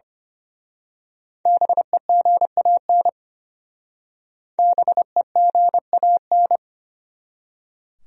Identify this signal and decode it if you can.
Morse code, heard as a keyed tone, about 28 wpm, 700 Hz. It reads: BEGAN BEGAN
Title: Speed-Racing Top 500 Words 25wpm